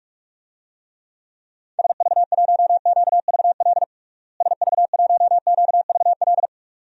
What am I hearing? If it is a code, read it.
Morse code, heard as a keyed tone, about 47 wpm, 700 Hz. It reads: SV1QVL SV1QVL
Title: DXCC-100 Call Signs - 6 Characters - Repeated 2x - 45wpm